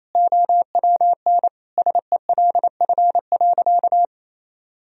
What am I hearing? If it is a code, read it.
Morse code, heard as a keyed tone, about 28 wpm, 700 Hz. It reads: OWN SELF.
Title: Sentences from Top 1000 Words - Review Part 1 - 28wpm